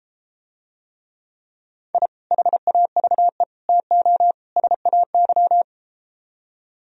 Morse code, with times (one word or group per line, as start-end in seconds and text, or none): 1.9-2.1 I
2.3-3.4 HAVE
3.7-4.3 TO
4.6-5.6 SAY